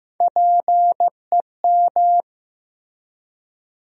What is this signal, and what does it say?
Morse code, heard as a keyed tone, about 15 wpm, 700 Hz. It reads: PEM
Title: USA Call Sign Postfixes - 3 Letters 15wpm